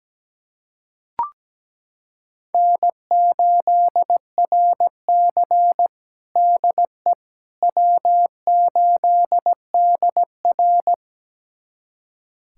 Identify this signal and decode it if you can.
Morse code, heard as a keyed tone, about 17 wpm, 700 Hz. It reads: N8RC DE W8DR
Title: USA Callsign DE Callsign - 4 Characters - 17wpm